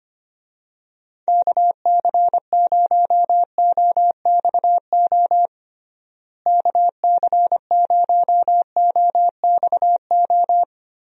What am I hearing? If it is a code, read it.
Morse code, heard as a keyed tone, about 25 wpm, 700 Hz. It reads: KC0OXO KC0OXO
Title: USA Call Signs - 6 Characters - Repeated 2x - 25wpm